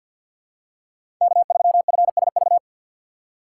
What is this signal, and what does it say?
Morse code, heard as a keyed tone, about 51 wpm, 700 Hz. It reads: K3FSV
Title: USA Call Signs - 2x Character Spacing - 50wpm